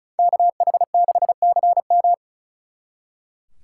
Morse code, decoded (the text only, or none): KH6CM